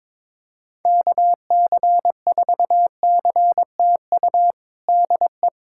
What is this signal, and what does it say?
Morse code, heard as a keyed tone, about 22 wpm, 700 Hz. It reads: KC4CTU DE
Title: USA Callsign DE Callsign - 6 Characters - 22wpm